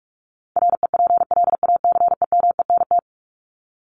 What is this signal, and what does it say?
Morse code, heard as a keyed tone, about 44 wpm, 700 Hz. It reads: REPLACEMENT
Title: Top 1751-2000 Words 45wpm